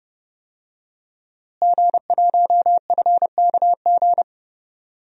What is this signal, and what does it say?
Morse code, heard as a keyed tone, about 30 wpm, 700 Hz. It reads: G1FKG
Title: DXCC-100 Call Signs - 5 Characters - Repeated 2x - 30wpm